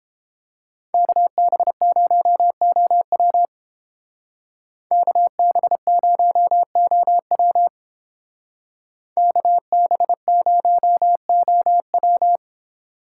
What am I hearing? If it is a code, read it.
Morse code, heard as a keyed tone, about 31 wpm, 700 Hz. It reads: KB0OW KB0OW KB0OW